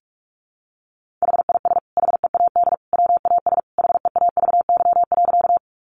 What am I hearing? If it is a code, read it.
Morse code, heard as a keyed tone, about 44 wpm, 700 Hz. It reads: HIS HEAD WAS HEAVY.
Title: Sentences from Top 1100 Words 45wpm